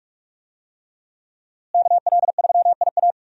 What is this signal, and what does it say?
Morse code, heard as a keyed tone, about 42 wpm, 700 Hz. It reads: KL3IA
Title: USA Call Signs - 5 Characters - Repeated 3x - 45wpm